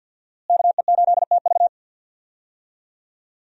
Morse code, heard as a keyed tone, about 46 wpm, 700 Hz. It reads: KE8TV
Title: USA Call Signs - 5 Characters - Repeated 2x - 50wpm